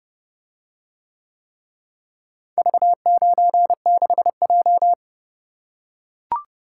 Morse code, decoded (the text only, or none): V96J